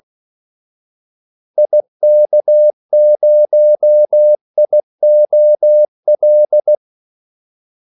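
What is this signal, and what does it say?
Morse code, heard as a keyed tone, about 16 wpm, 600 Hz. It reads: IK0IOL
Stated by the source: Morse call sign practice at 16 wpm.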